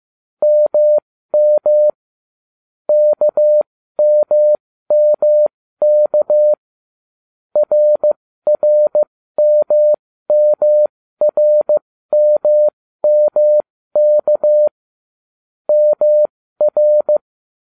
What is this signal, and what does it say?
Morse code, heard as a keyed tone, about 15 wpm, spaced as for 13 wpm, 600 Hz. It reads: MM KMMK RRMMRMMK MR